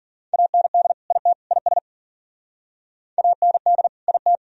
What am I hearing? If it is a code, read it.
Morse code, heard as a keyed tone, about 55 wpm, 700 Hz. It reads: AND IT IS AND IT